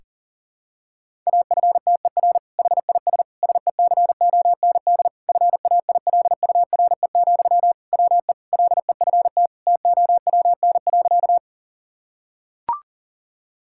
Morse code, heard as a keyed tone, about 40 wpm, 700 Hz. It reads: AFTER HIS SECOND FAILURE, WE LEFT TOWN.
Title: Sentences from Top 1000 Words - Review Part 6 - 40wpm